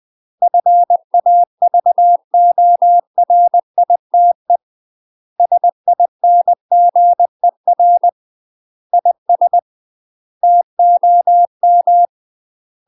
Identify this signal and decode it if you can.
Morse code, heard as a keyed tone, about 20 wpm, 700 Hz. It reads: FAVORITE SINGER IS TOM
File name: UK MASTER QSOs TRAINING / 20WPM / UK MASTER QSOs 0023